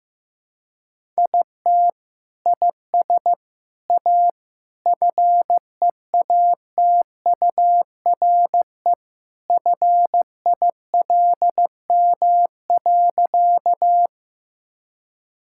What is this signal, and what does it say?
Morse code, heard as a keyed tone, about 15 wpm, 700 Hz. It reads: IT IS A FEATURE FILM.
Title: Sentences from Top 1000 Words - Review Part 5 - 15wpm